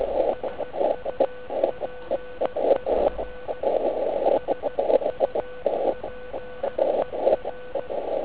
Morse code, heard as a keyed tone, about 16 wpm, 1500 Hz, with noise. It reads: SUPER SUPER